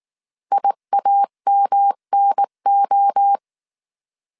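Morse code, heard as a keyed tone, about 19 wpm, 800 Hz. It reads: IAMNO
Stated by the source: Morse code dictation at 18 wpm from the IW3SGT QRS Morse code training course, five-character groups of 300 characters.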